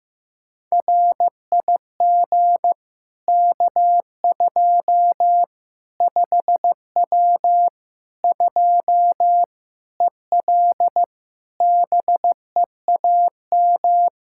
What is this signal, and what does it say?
Morse code, heard as a keyed tone, about 16 wpm, 700 Hz. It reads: RIG K2 5W 2 EL BEAM
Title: Practice QSO - Exchange 2 - Info - 15wpm